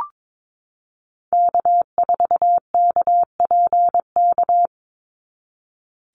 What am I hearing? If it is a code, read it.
Morse code, heard as a keyed tone, about 22 wpm, 700 Hz. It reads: K4KPK
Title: NA SOTA Activator Callsigns 22wpm